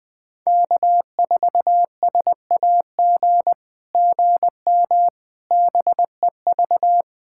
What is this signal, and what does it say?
Morse code, heard as a keyed tone, about 20 wpm, 700 Hz. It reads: K4SAG GM BEV
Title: SOTA QSO Practice 20wpm